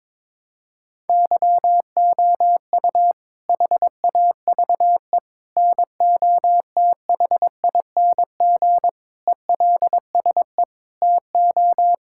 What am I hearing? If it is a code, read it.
Morse code, heard as a keyed tone, about 22 wpm, 700 Hz. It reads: YOU HAVE NOTHING ELSE TO
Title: Sentences from Top 1000 Words - Review Part 6 - 22wpm